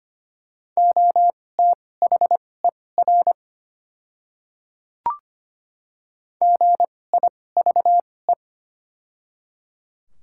Morse code, heard as a keyed tone, about 25 wpm, spaced as for 18 wpm, 690 Hz. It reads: OTHER GIVE
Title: 2x Character Spacing - Top 200 Words - 25wpm